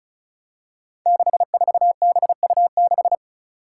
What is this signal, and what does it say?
Morse code, heard as a keyed tone, about 36 wpm, 700 Hz. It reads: 64BU6
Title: Five Challenging Letter-Number Combinations 35wpm